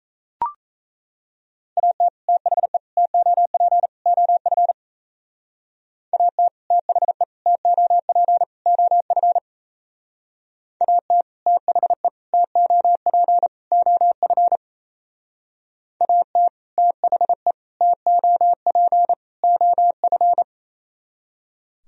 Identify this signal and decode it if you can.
Morse code, heard as a keyed tone, about 35 wpm, 700 Hz. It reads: AT THE TOP OF AT THE TOP OF AT THE TOP OF AT THE TOP OF
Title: Speed-Racing - Sets of 4 Words 28wpm